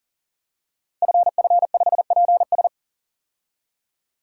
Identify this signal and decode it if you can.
Morse code, heard as a keyed tone, about 39 wpm, 700 Hz. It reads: FF5PS
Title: Code Groups - 5 Letter:Numbers - 40wpm